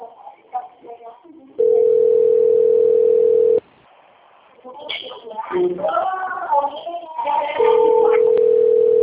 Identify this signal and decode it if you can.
Ringback tone (North America), 440 + 480 Hz, with speech.
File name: out- - - - - .35163